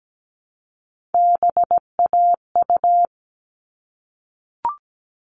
Morse code, heard as a keyed tone, about 17 wpm, 700 Hz. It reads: BAU
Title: All Cut Numbers - 3-Digit 17wpm